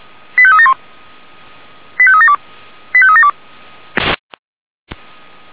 Five-tone selective call (ZVEI), heard as quick runs of five tones, tones of about 70 ms. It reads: 85371 85372 85372